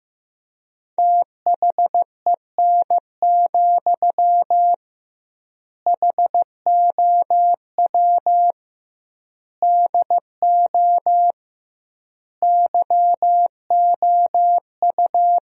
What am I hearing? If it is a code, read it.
Morse code, heard as a keyed tone, about 15 wpm, 700 Hz. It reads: THEN, HOW DO YOU
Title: 2x Word-Spacing - Sentences from Top 100 Words - 15wpm